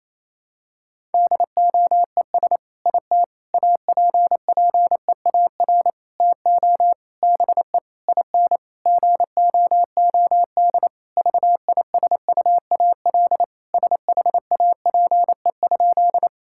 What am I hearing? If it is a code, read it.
Morse code, heard as a keyed tone, about 28 wpm, 700 Hz. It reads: DOES IT APPEAR TO BE IN GOOD VISUAL SHAPE?
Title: Sentences from Top 1000 Words - Review Part 6 - 28wpm